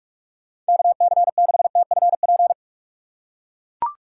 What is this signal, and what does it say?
Morse code, heard as a keyed tone, about 43 wpm, 700 Hz. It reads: KC6TFP
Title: USA Call Signs - 6 Characters - 8x Character Spacing - 45wpm